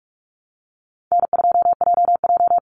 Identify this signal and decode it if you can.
Morse code, heard as a keyed tone, about 44 wpm, 700 Hz. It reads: N2JJ
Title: Call Signs - 45wpm